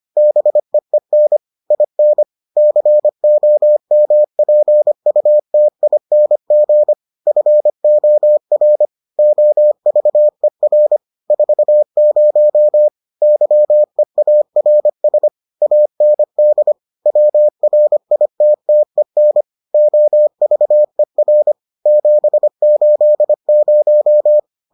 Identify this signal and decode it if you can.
Morse code, heard as a keyed tone, about 25 wpm, 600 Hz. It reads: BEEN IN COMPUTING FOR OVER 40 YEARS AND WRITTEN OVER 780